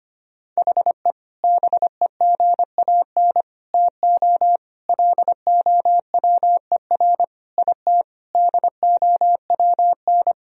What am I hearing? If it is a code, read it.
Morse code, heard as a keyed tone, about 25 wpm, 700 Hz. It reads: HE BEGAN TO LOWER IT DOWN
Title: Sentences from Top 1000 Words - Review Part 4 - 25wpm